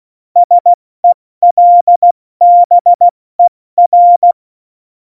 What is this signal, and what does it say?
Morse code, heard as a keyed tone, about 16 wpm, 700 Hz. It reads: SELBER
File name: Morsekurs / Witze / Witz 08. Dezember 2020, 70 F80